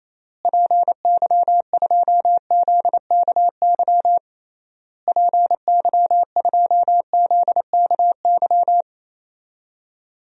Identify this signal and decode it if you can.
Morse code, heard as a keyed tone, about 28 wpm, 700 Hz. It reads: PY2ZKY PY2ZKY
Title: DXCC-100 Call Signs - 6 Characters - Repeated 3x - 28wpm